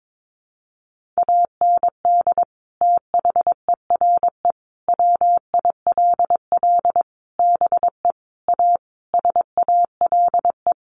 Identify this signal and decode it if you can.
Morse code, heard as a keyed tone, about 22 wpm, 700 Hz. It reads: AND THERE WILL BE A SALE
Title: Sentences from Top 1400 Words 22wpm